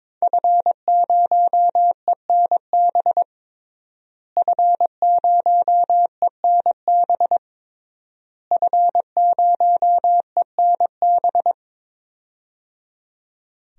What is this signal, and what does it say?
Morse code, heard as a keyed tone, about 21 wpm, 700 Hz. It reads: F0ENB F0ENB F0ENB
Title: DXCC-100 Call Signs - 5 Characters - Repeated 3x - 22wpm